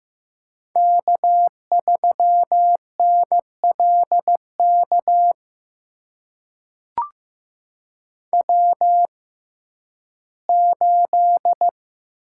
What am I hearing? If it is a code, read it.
Morse code, heard as a keyed tone, about 15 wpm, spaced as for 11 wpm, 700 Hz. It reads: K3NLK W8
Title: USA Call Signs - 5 Characters - 6x Character Spacing - 15wpm